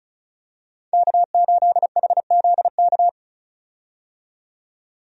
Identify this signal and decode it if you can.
Morse code, heard as a keyed tone, about 34 wpm, 700 Hz. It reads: K8HZK
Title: USA Call Signs - 5 Characters - 35wpm